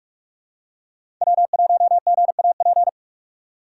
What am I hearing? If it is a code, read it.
Morse code, heard as a keyed tone, about 46 wpm, 700 Hz. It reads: W1GAP
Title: USA Call Signs - 5 Characters - 4x Character Spacing - 45wpm